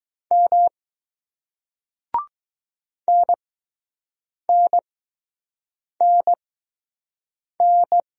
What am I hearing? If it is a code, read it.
Morse code, heard as a keyed tone, about 20 wpm, spaced as for 6 wpm, 700 Hz. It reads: M NNNN